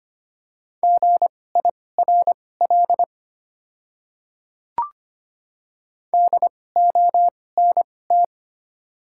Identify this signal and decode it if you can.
Morse code, heard as a keyed tone, about 25 wpm, spaced as for 18 wpm, 700 Hz. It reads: GIRL DONT
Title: 2x Character Spacing - Top 500 Words - 25wpm